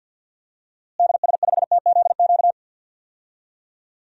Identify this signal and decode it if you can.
Morse code, heard as a keyed tone, about 49 wpm, 700 Hz. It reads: DS5TZQ
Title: DXCC-100 Call Signs - 6 Characters - Repeated 3x - 50wpm